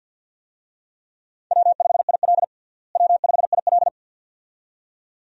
Morse code, heard as a keyed tone, about 52 wpm, 700 Hz. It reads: W5IL W5IL
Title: USA Call Signs - 4 Characters - Repeated 2x - 50wpm